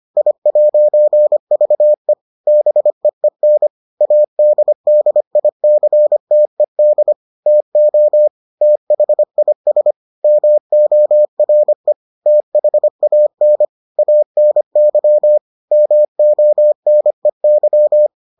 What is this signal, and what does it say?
Morse code, heard as a keyed tone, about 25 wpm, 600 Hz. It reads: I'VE BEEN ADDICTED TO THIS MORE THAN ANY MONEY